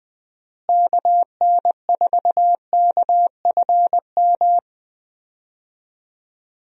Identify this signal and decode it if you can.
Morse code, heard as a keyed tone, about 20 wpm, 700 Hz. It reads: KN4KFM
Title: USA Call Signs - 6 Characters - Repeated 3x - 20wpm